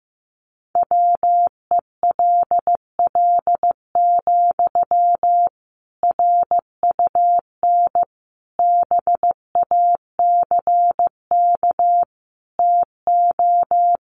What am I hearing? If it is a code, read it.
Morse code, heard as a keyed tone, about 15 wpm, 700 Hz. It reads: WELL, RUN BACK TO
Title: Sentences from Top 300 Words 15wpm